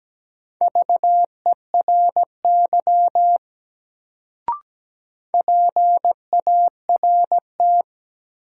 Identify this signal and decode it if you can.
Morse code, heard as a keyed tone, about 17 wpm, 700 Hz. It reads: VERY PART
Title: Sentences from Top 200 Words - New Word Drill - 17wpm